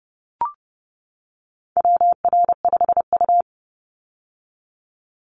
Morse code, heard as a keyed tone, about 30 wpm, 700 Hz. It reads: WR5U
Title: Call Signs - 30wpm